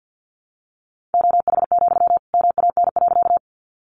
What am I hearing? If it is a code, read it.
Morse code, heard as a keyed tone, about 50 wpm, 700 Hz. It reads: OH, MAN.
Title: Sentences from Top 500 Words 50wpm